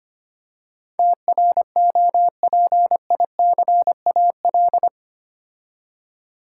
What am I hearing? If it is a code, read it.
Morse code, heard as a keyed tone, about 25 wpm, 700 Hz. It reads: TROPICAL
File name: 8 Letter Words - First 3 Letters Given - 25wpm